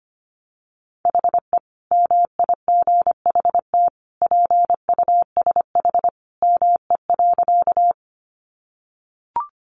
Morse code, heard as a keyed tone, about 25 wpm, 700 Hz. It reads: HE MIGHT PUSH ME.
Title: Sentences from Top 1300 Words 25wpm